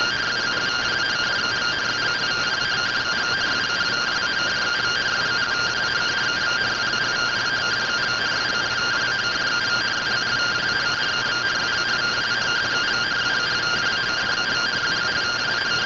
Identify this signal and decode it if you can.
Two-tone teleprinter signal, about 170 Hz shift.
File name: findings / CORONAL fr69 nb202 3mm.gen-08 rtty rx